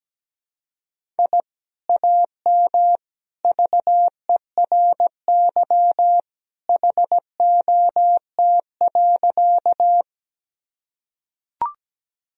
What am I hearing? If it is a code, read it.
Morse code, heard as a keyed tone, about 17 wpm, 700 Hz. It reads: I AM VERY HOT.
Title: Sentences from Top 1000 Words - Review Part 3 - 17wpm